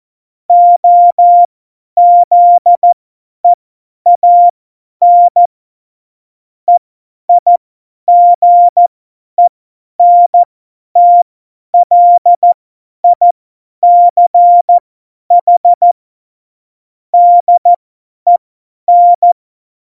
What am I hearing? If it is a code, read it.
Morse code, heard as a keyed tone, about 14 wpm, spaced as for 10 wpm, 700 Hz. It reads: OZEAN EIGENTLICH DEN
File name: Morsekurs / Witze / Witz 16. April 2019, 50 F70